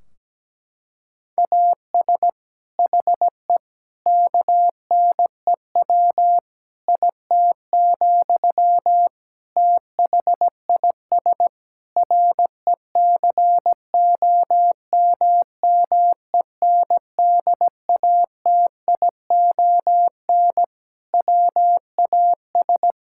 Morse code, heard as a keyed tone, about 17 wpm, 700 Hz. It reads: AS HE KNEW IT, THIS RECOMMENDATION WAS
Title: Sentences from Top 1600 Words 17wpm